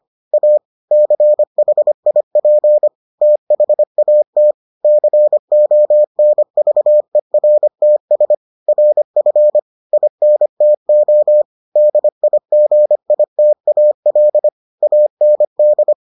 Morse code, heard as a keyed tone, about 25 wpm, 600 Hz. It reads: A CHIP THAT CONVERTS RF INTO DIGITAL AND